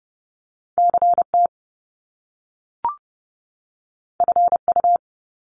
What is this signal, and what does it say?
Morse code, heard as a keyed tone, about 30 wpm, 700 Hz. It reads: CT FU
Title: Top Two Letter Combinations in Words 30wpm